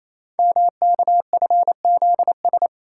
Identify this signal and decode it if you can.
Morse code, heard as a keyed tone, about 29 wpm, 700 Hz. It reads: MKFZS
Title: Code Groups - 5 Letters - 28wpm